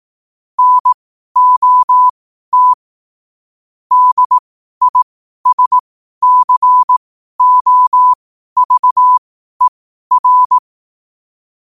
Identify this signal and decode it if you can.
Morse code, heard as a keyed tone, about 18 wpm, spaced as for 12 wpm, 1000 Hz. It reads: NOT DISCOVER
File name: qotdfiles / 12wpm